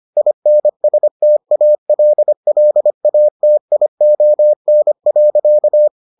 Morse code, heard as a keyed tone, about 25 wpm, 600 Hz. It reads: INSTALLATION.